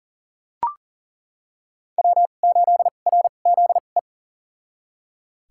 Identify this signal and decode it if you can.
Morse code, heard as a keyed tone, about 37 wpm, spaced as for 28 wpm, 690 Hz. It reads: W8RZE
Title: USA Call Signs - 2x Character Spacing - 40wpm